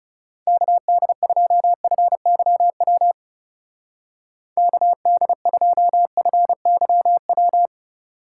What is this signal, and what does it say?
Morse code, heard as a keyed tone, about 36 wpm, 700 Hz. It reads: KD2FYW KD2FYW
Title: Speed Racing - USA 2x3 Call Signs 30wpm